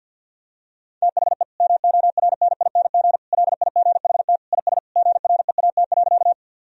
Morse code, heard as a keyed tone, about 51 wpm, 700 Hz. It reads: THE MORNING LIGHT IS GREAT.
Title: Sentences from Top 1000 Words - Review Part 1 - 50wpm